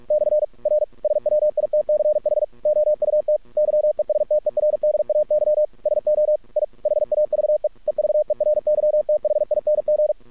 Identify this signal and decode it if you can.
Morse code, heard as a keyed tone, about 46 wpm, 600 Hz. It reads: / R SWITCH OUT YESTERDAY SO I HAVE EVERYTHING
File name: archived / CRSnet